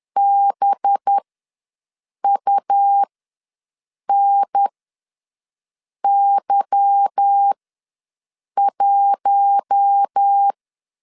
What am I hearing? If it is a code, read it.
Morse code, heard as a keyed tone, about 10 wpm, spaced as for 6 wpm, 800 Hz. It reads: BUNY1